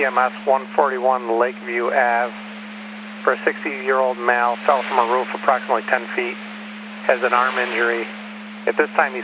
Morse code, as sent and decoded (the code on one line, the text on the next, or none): none